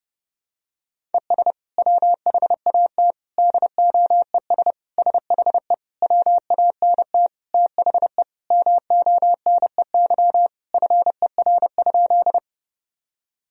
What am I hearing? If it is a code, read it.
Morse code, heard as a keyed tone, about 30 wpm, 700 Hz. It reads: ES WHAT DOES SHE WANT THE MONEY FER?
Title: Sentences from Top 400 Words and Abbreviations 30wpm